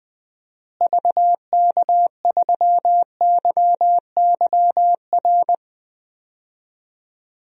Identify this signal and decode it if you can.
Morse code, heard as a keyed tone, about 20 wpm, 700 Hz. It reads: VK3YYR
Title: DXCC-100 Call Signs - 6 Characters - Repeated 2x - 20wpm